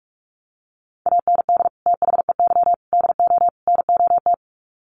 Morse code, heard as a keyed tone, about 44 wpm, 700 Hz. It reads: AND THEY DO NOT
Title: Sets of 4 Words 45wpm